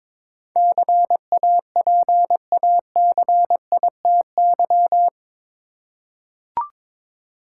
Morse code, heard as a keyed tone, about 21 wpm, 700 Hz. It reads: CAPACITY
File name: practice / 22wpm / Eight Letter Words - 22wpm